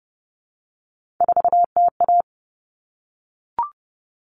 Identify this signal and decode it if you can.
Morse code, heard as a keyed tone, about 30 wpm, 700 Hz. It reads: 4TA
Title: All Cut Numbers - 3-Digit 30wpm